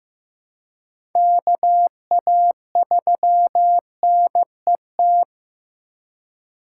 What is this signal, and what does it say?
Morse code, heard as a keyed tone, about 15 wpm, 700 Hz. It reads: KA3NET
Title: USA Call Signs - 6 Characters - 3x Character Spacing - 15wpm